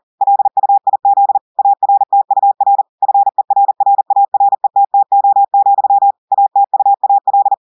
Morse code, heard as a keyed tone, about 40 wpm, 800 Hz. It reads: LUIZ ARTUR FERRARETTO, ATUAL
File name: audios / portugais / portugais texte en clair 1 vitesse 40 mots minutes